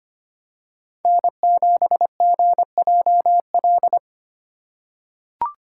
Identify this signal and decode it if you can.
Morse code, heard as a keyed tone, about 25 wpm, 700 Hz. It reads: N7GJL
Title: USA Call Signs - 5 Characters - 4x Character Spacing - 25wpm